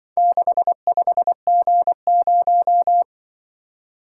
Morse code, heard as a keyed tone, about 23 wpm, 700 Hz. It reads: 65G0
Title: Speed-Racing - Four Letter-Number Combinations - 20wpm